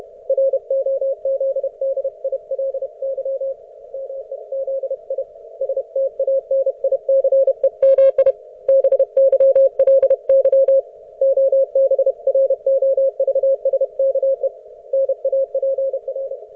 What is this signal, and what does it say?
Morse code, heard as a keyed tone, about 32 wpm, 540 Hz. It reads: ROZDILY MEZI STANICEMI BYLY OBROVSKE NAPR